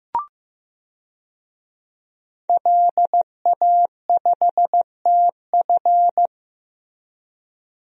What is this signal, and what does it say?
Morse code, heard as a keyed tone, about 15 wpm, 700 Hz. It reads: LA5TF
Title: DXCC-100 Call Signs - 5 Characters - Repeated 2x - 15wpm